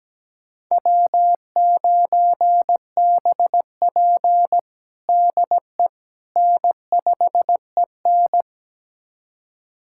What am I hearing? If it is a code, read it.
Morse code, heard as a keyed tone, about 17 wpm, 700 Hz. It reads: W9BP DE N5EN
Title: USA Callsign DE Callsign - 4 Characters - 17wpm